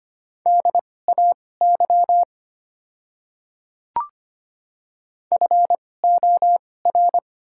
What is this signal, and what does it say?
Morse code, heard as a keyed tone, about 25 wpm, spaced as for 18 wpm, 700 Hz. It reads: DAY FOR
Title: 2x Character Spacing - Top 100 Words - 25wpm